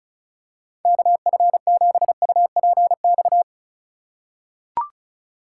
Morse code, heard as a keyed tone, about 34 wpm, 700 Hz. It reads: KF7UPX
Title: USA Call Signs - 6 Characters - 4x Character Spacing - 35wpm